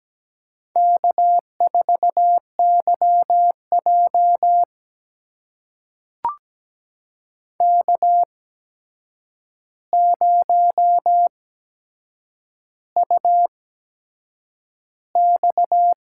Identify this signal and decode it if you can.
Morse code, heard as a keyed tone, about 17 wpm, spaced as for 7 wpm, 700 Hz. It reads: K4YJ K0UX
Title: USA Call Signs - 4 Characters - 8x Character Spacing - 17wpm